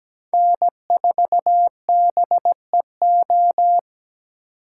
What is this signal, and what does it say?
Morse code, heard as a keyed tone, about 20 wpm, 700 Hz. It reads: N4BEO
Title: Speed Racing - USA 1x3 Call Signs 17wpm